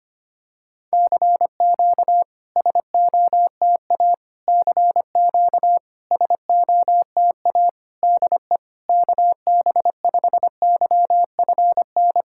Morse code, heard as a keyed tone, about 25 wpm, 700 Hz. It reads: CQ SOTA CQ SOTA DE KB5YFN
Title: SOTA QSO Practice 25wpm